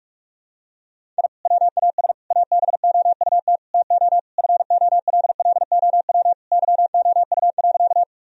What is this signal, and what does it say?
Morse code, heard as a keyed tone, about 41 wpm, 700 Hz. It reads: I WAS ABOUT TO FOLLOW YOU.
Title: Sentences from Top 1000 Words - Review Part 1 - 45wpm